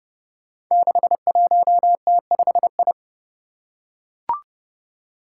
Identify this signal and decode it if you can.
Morse code, heard as a keyed tone, about 30 wpm, 700 Hz. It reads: 61T5I
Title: Five Challenging Letter-Number Combinations 30wpm